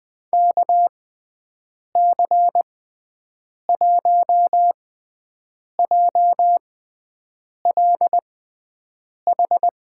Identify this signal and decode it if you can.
Morse code, heard as a keyed tone, about 20 wpm, spaced as for 7 wpm, 700 Hz. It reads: KC1JLH
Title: USA Call Signs - 6 Characters - 6x Character Spacing - 20wpm